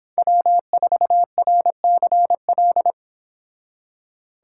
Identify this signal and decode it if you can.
Morse code, heard as a keyed tone, about 26 wpm, 700 Hz. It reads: W4RCL